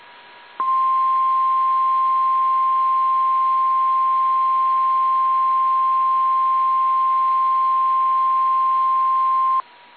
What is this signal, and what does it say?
Emergency Alert System (SAME) tones (the 1050 Hz attention tone). It is heard from 0.6 to 9.7 s.